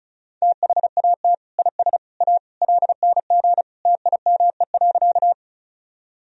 Morse code, heard as a keyed tone, about 34 wpm, 700 Hz. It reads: THAT IS A LNG TIME.
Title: Sentences from Top 100 Words and Abbreviations 35wpm